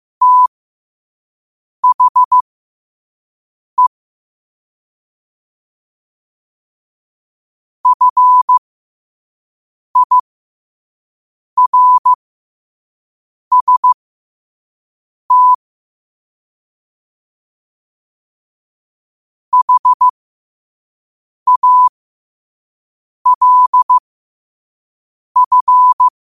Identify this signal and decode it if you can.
Morse code, heard as a keyed tone, about 15 wpm, spaced as for 5 wpm, 1000 Hz. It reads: THE FIRST HALF
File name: qotdfiles / 5wpm